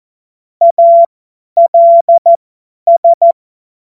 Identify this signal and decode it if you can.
Morse code, heard as a keyed tone, about 14 wpm, spaced as for 10 wpm, 680 Hz. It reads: ALS
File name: Morsekurs / Witze / Witz 02. August 2022 50 F70